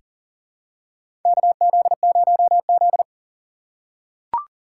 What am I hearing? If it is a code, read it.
Morse code, heard as a keyed tone, about 40 wpm, 700 Hz. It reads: KZ0Z
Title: USA Call Signs - 4 Characters - 4x Character Spacing - 40wpm